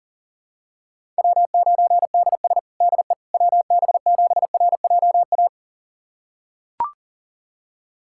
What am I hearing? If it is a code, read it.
Morse code, heard as a keyed tone, about 40 wpm, 700 Hz. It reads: W9DS DE WB7RJA
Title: USA Callsign DE Callsign 40wpm